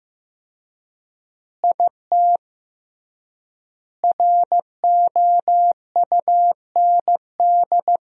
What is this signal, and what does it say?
Morse code, heard as a keyed tone, about 15 wpm, 700 Hz. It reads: IT ROUND
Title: 3x Word-Spacing - Sentences from Top 500 Words - 15wpm